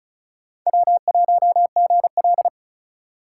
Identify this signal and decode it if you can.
Morse code, heard as a keyed tone, about 36 wpm, 700 Hz. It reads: W1GL
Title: USA Call Signs - 4 Characters - 3x Character Spacing - 35wpm